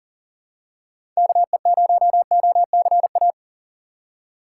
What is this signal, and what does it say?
Morse code, heard as a keyed tone, about 40 wpm, 700 Hz. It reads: KE0OCA
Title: USA Call Signs - 6 Characters - 2x Character Spacing - 40wpm